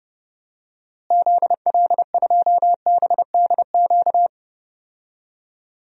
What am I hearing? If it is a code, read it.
Morse code, heard as a keyed tone, about 30 wpm, 700 Hz. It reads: ZL2BDQ
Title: DXCC-100 Call Signs - 6 Characters - Repeated 2x - 30wpm